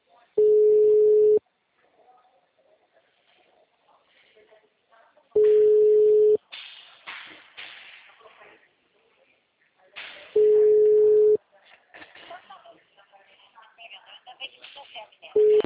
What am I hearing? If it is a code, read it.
Ringback tone (Europe, 425 Hz), with speech.
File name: out- -2020- - - .18262